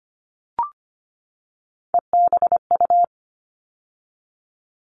Morse code, heard as a keyed tone, about 25 wpm, 700 Hz. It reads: EBU
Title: All Cut Numbers - 3-Digit 25wpm